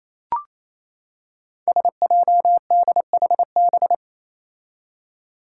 Morse code, heard as a keyed tone, about 28 wpm, 700 Hz. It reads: SJDHB
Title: Five Challenging Letter-Number Combinations 28wpm